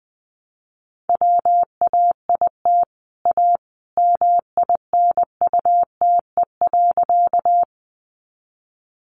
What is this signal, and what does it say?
Morse code, heard as a keyed tone, about 20 wpm, 700 Hz. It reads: WAIT A MINUTE.
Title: Sentences from Top 500 Words 20wpm